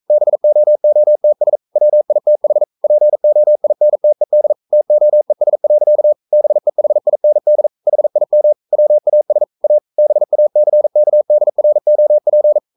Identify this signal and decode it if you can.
Morse code, heard as a keyed tone, about 42 wpm, 600 Hz. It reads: BOOTS WITH POINTED TOES. BEHIND HIM WAS A BACKDROP
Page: Romanen Never af Ken Follet på engelsk.